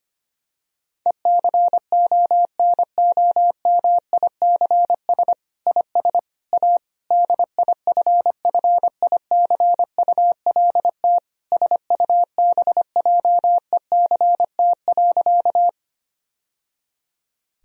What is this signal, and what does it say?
Morse code, heard as a keyed tone, about 25 wpm, 700 Hz. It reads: ECONOMICS IS A DIFFICULT SUBJECT.
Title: Sentences from Top 1000 Words - Review Part 3 - 25wpm